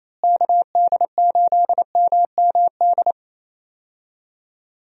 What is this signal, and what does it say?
Morse code, heard as a keyed tone, about 28 wpm, 700 Hz. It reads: KD8MMD
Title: USA Call Signs - 6 Characters - Repeated 3x - 28wpm